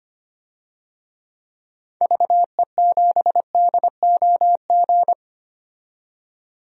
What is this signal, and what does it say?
Morse code, heard as a keyed tone, about 25 wpm, 700 Hz. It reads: VE7DOG
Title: DXCC-100 Call Signs - 6 Characters - 25wpm